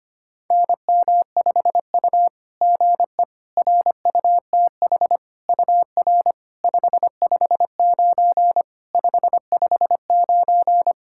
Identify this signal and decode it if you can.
Morse code, heard as a keyed tone, about 25 wpm, 700 Hz. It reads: NM5U GE RUTH UR 559 559
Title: SOTA QSO Practice 25wpm